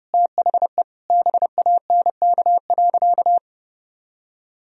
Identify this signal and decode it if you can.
Morse code, heard as a keyed tone, about 32 wpm, 700 Hz. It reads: THE BANK.
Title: Sentences from Top 1000 Words - Review Part 4 - 30wpm